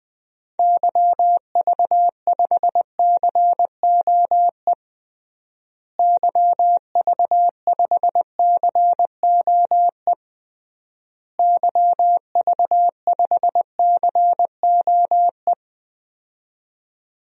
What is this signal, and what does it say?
Morse code, heard as a keyed tone, about 19 wpm, 700 Hz. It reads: YV5COE YV5COE YV5COE
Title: DXCC-100 Call Signs - 6 Characters - Repeated 3x - 20wpm